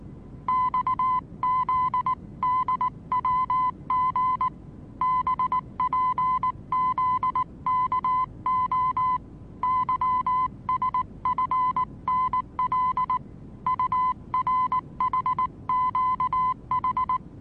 Morse code, heard as a keyed tone, about 19 wpm, 1010 Hz. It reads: XZDWG BPZKO YSFNL URHQH
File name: repos / Morsekurs / DJ4UF Morsen26